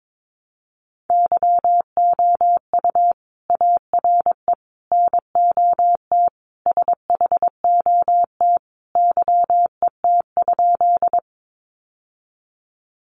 Morse code, heard as a keyed tone, about 22 wpm, 700 Hz. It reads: YOU ARE NOT SHOT YET?
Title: Sentences from Top 1100 Words 22wpm